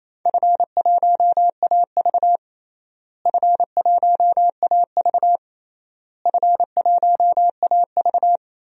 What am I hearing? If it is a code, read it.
Morse code, heard as a keyed tone, about 27 wpm, 700 Hz. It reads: F1AV F1AV F1AV
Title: DXCC-100 Call Signs - 4 Characters - Repeated 3x - 28wpm